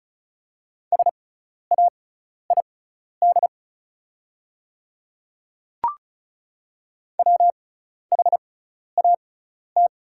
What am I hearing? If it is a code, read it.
Morse code, heard as a keyed tone, about 36 wpm, spaced as for 12 wpm, 700 Hz. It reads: SAID WHAT
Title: 6x Character Spacing - Top 100 Words - 35wpm